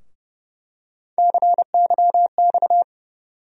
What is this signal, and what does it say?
Morse code, heard as a keyed tone, about 31 wpm, 700 Hz. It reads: CYX
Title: USA Call Sign Postfixes - 3 Letters 30wpm